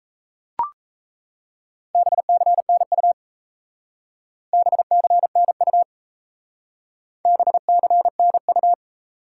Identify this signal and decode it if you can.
Morse code, heard as a keyed tone, about 34 wpm, 700 Hz. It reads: BCNU BCNU BCNU
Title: Speed-Racing Top 90 QSO Elements 28wpm